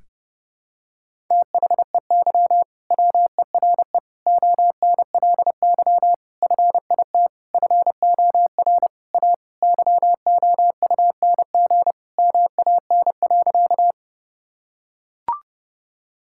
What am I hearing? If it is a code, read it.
Morse code, heard as a keyed tone, about 32 wpm, 700 Hz. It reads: THEY WERE ONLY FIT FOR A YOUNG MAN.
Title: Sentences from Top 1000 Words - Review Part 5 - 30wpm